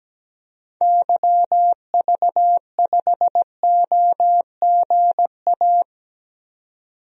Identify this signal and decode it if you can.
Morse code, heard as a keyed tone, about 17 wpm, 700 Hz. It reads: YV5OGA
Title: DXCC-100 Call Signs - 6 Characters - 17wpm